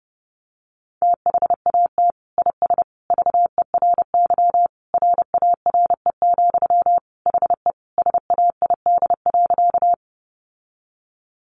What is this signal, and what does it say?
Morse code, heard as a keyed tone, about 30 wpm, 700 Hz. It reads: THAT IS VERY RARE, HE SAID.
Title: Sentences from Top 1100 Words 30wpm